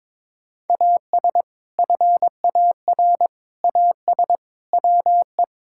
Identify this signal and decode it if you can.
Morse code, heard as a keyed tone, about 22 wpm, 700 Hz. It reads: AS FAR AS WE
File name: practice / 22wpm / Sets of 6 Words - 22wpm